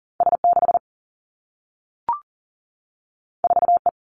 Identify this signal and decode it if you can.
Morse code, heard as a keyed tone, about 39 wpm, 700 Hz. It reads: S 6 4 E
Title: Two Challenging Letter-Number Combinations 40wpm